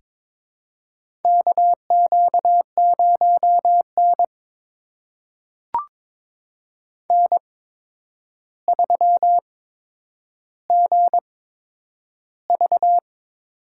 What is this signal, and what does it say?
Morse code, heard as a keyed tone, about 22 wpm, spaced as for 9 wpm, 700 Hz. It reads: KQ0N N3GV
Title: USA Call Signs - 8x Character Spacing - 22wpm